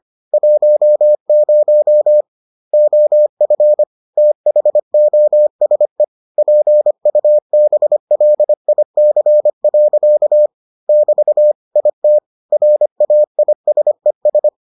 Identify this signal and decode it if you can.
Morse code, heard as a keyed tone, about 25 wpm, 600 Hz. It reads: 10 OF THOSE PUBLIC. <BT> IT RAISES